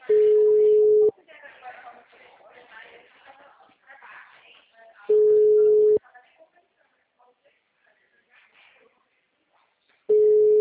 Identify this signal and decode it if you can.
Ringback tone (Europe, 425 Hz), with speech.
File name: out- -2014- - - .59936